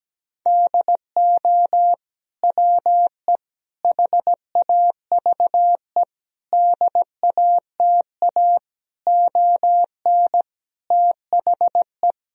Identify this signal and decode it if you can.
Morse code, heard as a keyed tone, about 17 wpm, 700 Hz. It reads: DO WE HAVE DATA ON THE
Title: Sentences from Top 1000 Words - Review Part 3 - 17wpm